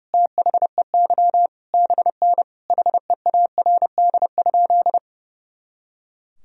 Morse code, heard as a keyed tone, about 32 wpm, 700 Hz. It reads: THEY BN HEARD?
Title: Sentences from Top 400 Words and Abbreviations 30wpm